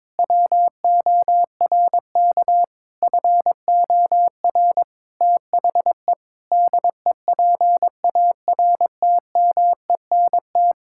Morse code, heard as a keyed tone, about 22 wpm, 700 Hz. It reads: WORK FOR THE DEPARTMENT
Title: Sentences from Top 1000 Words - Review Part 5 - 22wpm